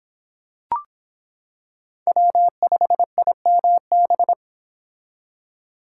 Morse code, heard as a keyed tone, about 26 wpm, 700 Hz. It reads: W5IMB